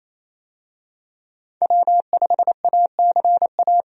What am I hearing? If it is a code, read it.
Morse code, heard as a keyed tone, about 27 wpm, 700 Hz. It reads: W5ACA
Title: USA Call Signs - 5 Characters - 28wpm